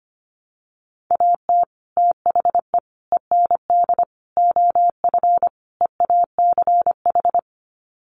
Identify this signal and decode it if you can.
Morse code, heard as a keyed tone, about 25 wpm, 700 Hz. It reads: AT THE END OF EACH
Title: Sets of 5 Words Encore 25wpm